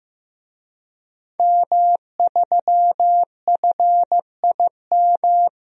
Morse code, heard as a keyed tone, about 16 wpm, 700 Hz. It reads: M3FIM
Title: DXCC-100 Call Signs - 5 Characters - 15wpm